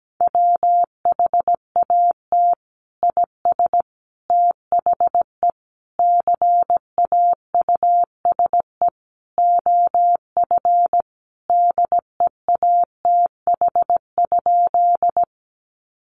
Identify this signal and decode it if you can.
Morse code, heard as a keyed tone, about 17 wpm, 700 Hz. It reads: WHAT IS THE CAUSE OF DEATH?
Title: Sentences from Top 1100 Words 17wpm